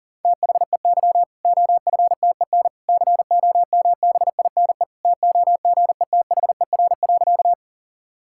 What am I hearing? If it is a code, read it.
Morse code, heard as a keyed tone, about 40 wpm, 700 Hz. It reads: THEY OFTEN COMBINE TOGETHER.
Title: Sentences from Top 1600 Words 40wpm